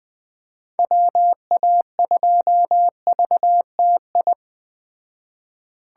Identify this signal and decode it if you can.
Morse code, heard as a keyed tone, about 20 wpm, 700 Hz. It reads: WA2VTI
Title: Speed Racing - USA 2x3 Call Signs 17wpm